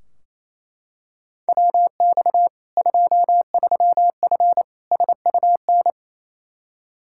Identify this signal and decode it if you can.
Morse code, heard as a keyed tone, about 28 wpm, 700 Hz. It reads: WX 23F SUN
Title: Practice QSO - Exchange 2 - Info - 28wpm